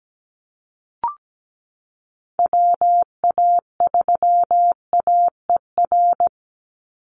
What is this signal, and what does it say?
Morse code, heard as a keyed tone, about 17 wpm, 700 Hz. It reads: WA3AER
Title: Call Signs Repeated 3x - 17wpm